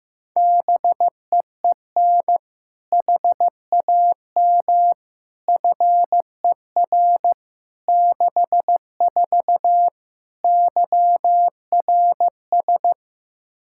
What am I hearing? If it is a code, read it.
Morse code, heard as a keyed tone, about 15 wpm, 700 Hz. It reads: BEEN HAM FER 64 YRS
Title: K1USN SST QSO Practice 15wpm